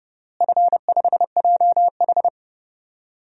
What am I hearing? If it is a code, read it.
Morse code, heard as a keyed tone, about 31 wpm, 700 Hz. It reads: F5JH
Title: DXCC-100 Call Signs - 4 Characters - Repeated 2x - 30wpm